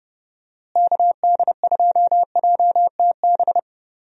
Morse code, heard as a keyed tone, about 30 wpm, 700 Hz. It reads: KD2JTB
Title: USA Call Signs - 6 Characters - Repeated 3x - 30wpm